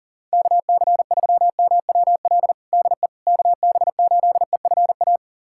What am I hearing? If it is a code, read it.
Morse code, heard as a keyed tone, about 40 wpm, 700 Hz. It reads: KC3MWL DE KB8EFA
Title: USA Callsign DE Callsign - 6 Characters - 40wpm